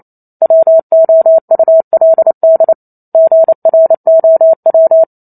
Morse code, heard as a keyed tone, about 28 wpm, 650 Hz. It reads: WOULD GROW
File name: Chapter 2 - 28 WPM